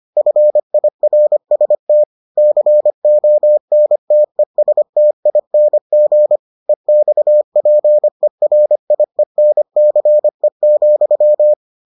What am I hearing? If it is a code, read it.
Morse code, heard as a keyed tone, about 25 wpm, 600 Hz. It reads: FIRST CONTESTING EXPERIENCE,